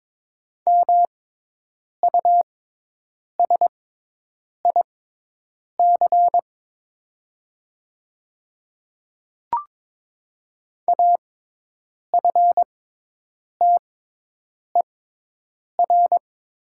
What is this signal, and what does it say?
Morse code, heard as a keyed tone, about 22 wpm, spaced as for 8 wpm, 700 Hz. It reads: MUSIC AFTER
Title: 6x Character Spacing - Top 300 Words - 22wpm